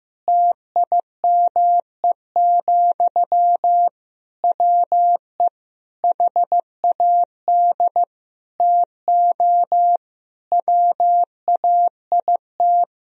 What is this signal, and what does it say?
Morse code, heard as a keyed tone, about 16 wpm, 700 Hz. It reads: TIME, WE HAD TO WAIT
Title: Sentences from Top 1000 Words - Review Part 2 - 15wpm